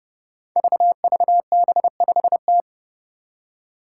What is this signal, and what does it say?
Morse code, heard as a keyed tone, about 30 wpm, 700 Hz. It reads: VVB5T
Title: Five Challenging Letter-Number Combinations 30wpm